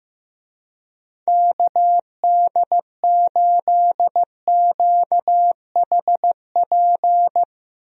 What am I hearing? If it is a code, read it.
Morse code, heard as a keyed tone, about 15 wpm, 700 Hz. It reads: KD8QHP
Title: USA Call Signs - 6 Characters - Repeated 2x - 15wpm